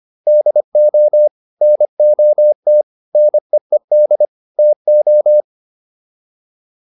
Morse code, heard as a keyed tone, about 25 wpm, 600 Hz. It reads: DO NOT NEED TO